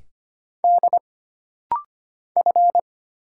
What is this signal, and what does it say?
Morse code, heard as a keyed tone, about 25 wpm, 700 Hz. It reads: D F
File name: Single Character (Letter:Number:Punctuation:Prosign) - Rapid-Fire - 25wpm